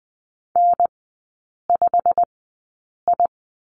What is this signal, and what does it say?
Morse code, heard as a keyed tone, about 20 wpm, spaced as for 8 wpm, 700 Hz. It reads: N5I
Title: 087 Lesson 15e - Number 5 - Three Characters 20wpm